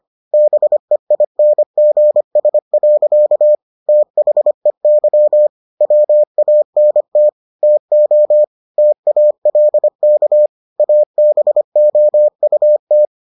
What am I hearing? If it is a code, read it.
Morse code, heard as a keyed tone, about 25 wpm, 600 Hz. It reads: BEINGS. THEY WANT TO TALK ABOUT